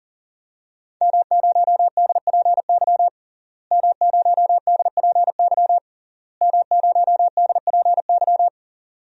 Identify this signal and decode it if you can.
Morse code, heard as a keyed tone, about 40 wpm, 700 Hz. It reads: M0DPY M0DPY M0DPY
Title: DXCC-100 Call Signs - 5 Characters - Repeated 3x - 40wpm